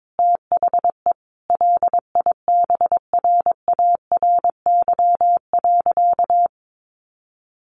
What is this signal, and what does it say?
Morse code, heard as a keyed tone, about 22 wpm, 700 Hz. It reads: THE LIBRARY.
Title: Sentences from Top 1400 Words 22wpm